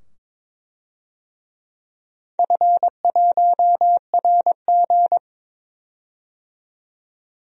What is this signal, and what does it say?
Morse code, heard as a keyed tone, about 21 wpm, 700 Hz. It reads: F1RG
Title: DXCC-100 Call Signs - 4 Characters - 22wpm